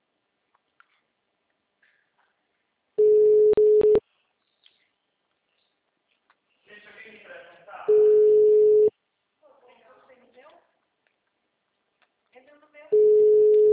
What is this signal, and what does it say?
Ringback tone (Europe, 425 Hz), with speech.
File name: out- -2016- - - .75